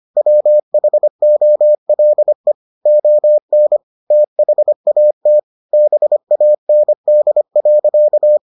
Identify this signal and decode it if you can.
Morse code, heard as a keyed tone, about 25 wpm, 600 Hz. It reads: WHOLE ON THAT BAND.